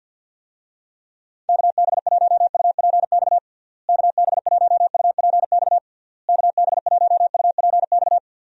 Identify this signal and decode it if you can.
Morse code, heard as a keyed tone, about 52 wpm, 700 Hz. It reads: KB1UPX KB1UPX KB1UPX
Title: USA Call Signs - 6 Characters - Repeated 3x - 50wpm